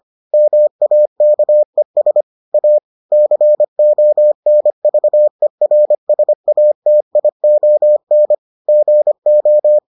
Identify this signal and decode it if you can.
Morse code, heard as a keyed tone, about 25 wpm, 600 Hz. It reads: MAKES A CONVERSATION GO